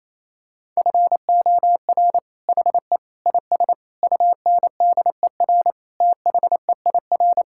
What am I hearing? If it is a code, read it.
Morse code, heard as a keyed tone, about 28 wpm, 700 Hz. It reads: FOR HE IS UNDER THEIR
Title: Sentences from Top 1600 Words 28wpm